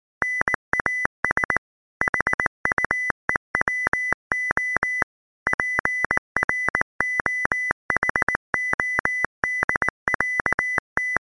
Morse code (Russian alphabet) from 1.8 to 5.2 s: ХУЕВО